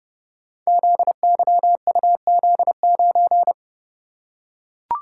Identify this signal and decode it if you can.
Morse code, heard as a keyed tone, about 30 wpm, 700 Hz. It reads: ZYUZ9
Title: Code Groups - 5 Letter:Numbers - 30wpm